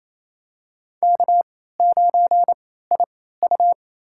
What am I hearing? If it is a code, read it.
Morse code, heard as a keyed tone, about 28 wpm, spaced as for 16 wpm, 700 Hz. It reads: K9IU
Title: USA Call Signs - 4 Characters - 3x Character Spacing - 28wpm